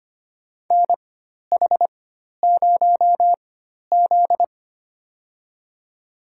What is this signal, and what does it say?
Morse code, heard as a keyed tone, about 25 wpm, spaced as for 12 wpm, 700 Hz. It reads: NH0Z
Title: USA Call Signs - 4x Character Spacing - 25wpm